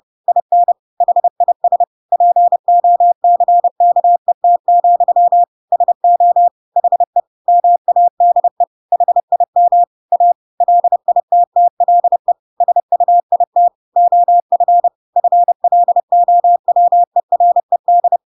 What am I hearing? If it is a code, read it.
Morse code, heard as a keyed tone, about 30 wpm, 700 Hz. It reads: IN HIS POCKET, SO HE MADE HIM A LITTLE SUIT OF FLOWERED
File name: Pinocchio / Pinocchio ch8 30 30 ik1hge